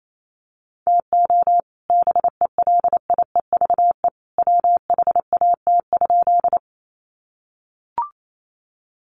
Morse code, heard as a keyed tone, about 28 wpm, 700 Hz. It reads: TO BELIEVE WHAT?
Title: Sentences from Top 1000 Words 28wpm